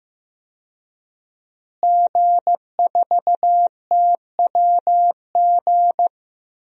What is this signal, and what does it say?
Morse code, heard as a keyed tone, about 15 wpm, 700 Hz. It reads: G4TWG
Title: DXCC-100 Call Signs - 5 Characters - Repeated 2x - 15wpm